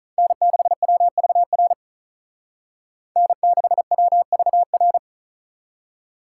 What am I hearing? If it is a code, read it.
Morse code, heard as a keyed tone, about 44 wpm, 700 Hz. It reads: N6WVR N6WVR
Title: Speed Racing - USA Call Signs 35wpm